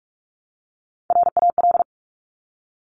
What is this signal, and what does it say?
Morse code, heard as a keyed tone, about 46 wpm, 690 Hz. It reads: RAL
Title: Top Three Letter Combinations in Words 45wpm